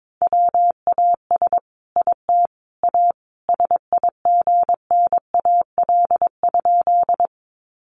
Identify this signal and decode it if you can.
Morse code, heard as a keyed tone, about 22 wpm, 700 Hz. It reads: WAS IT A SIGNAL?
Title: Sentences from Top 1400 Words 22wpm